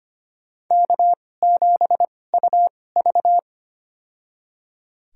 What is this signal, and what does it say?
Morse code, heard as a keyed tone, about 25 wpm, spaced as for 18 wpm, 700 Hz. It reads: K7UV
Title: USA Call Signs - 2x Character Spacing - 25wpm